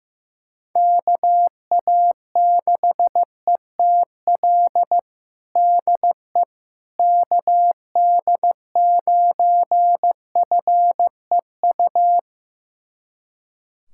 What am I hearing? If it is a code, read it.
Morse code, heard as a keyed tone, about 15 wpm, 700 Hz. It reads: KA6ETL DE KD9FEU
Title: USA Callsign DE Callsign - 6 Characters - 15wpm